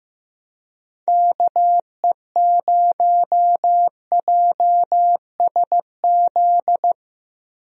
Morse code, heard as a keyed tone, about 15 wpm, 700 Hz. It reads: KE0JSZ
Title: USA Call Signs - 6 Characters - Repeated 2x - 15wpm